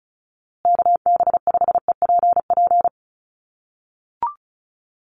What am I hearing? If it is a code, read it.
Morse code, heard as a keyed tone, about 34 wpm, 700 Hz. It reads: KB5EPP
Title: USA Call Signs Repeated 2x 35wpm